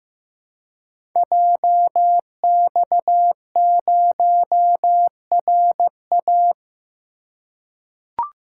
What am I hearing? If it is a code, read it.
Morse code, heard as a keyed tone, about 15 wpm, 700 Hz. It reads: JX0RA
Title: Code Groups - 5 Letter:Numbers - 15wpm